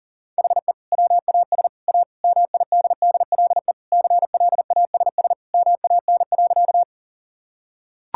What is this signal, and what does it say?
Morse code, heard as a keyed tone, about 37 wpm, 700 Hz. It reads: HE WAS A MIDDLE CLASS MAN.
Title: Sentences from Top 1000 Words - Review Part 6 - 40wpm